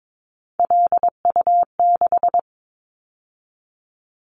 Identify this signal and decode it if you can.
Morse code, heard as a keyed tone, about 22 wpm, 700 Hz. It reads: LU6
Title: Callsign Prefixes for Top 50 DXCC - 22wpm